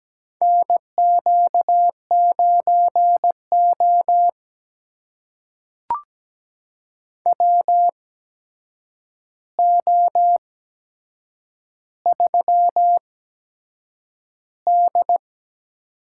Morse code, heard as a keyed tone, about 17 wpm, spaced as for 7 wpm, 700 Hz. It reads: NQ9O WO3D
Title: USA Call Signs - 4 Characters - 8x Character Spacing - 17wpm